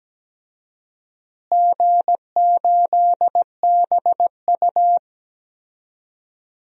Morse code, heard as a keyed tone, about 17 wpm, 700 Hz. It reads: G8BU
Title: DXCC-100 Call Signs - 4 Characters - Repeated 3x - 17wpm